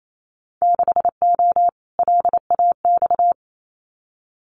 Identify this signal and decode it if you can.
Morse code, heard as a keyed tone, about 28 wpm, 700 Hz. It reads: 6O LAX
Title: Winter Field Day Exchanges Encore 28wpm